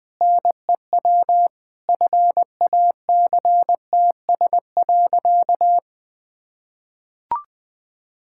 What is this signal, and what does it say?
Morse code, heard as a keyed tone, about 20 wpm, 700 Hz. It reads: NEW FACTS.
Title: Sentences with 5-Word Phrases - 20wpm